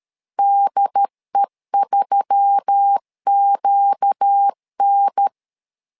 Morse code, heard as a keyed tone, about 13 wpm, 800 Hz. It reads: DE3QN
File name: LEZIONI / 4 Q1S3 12wpm 2023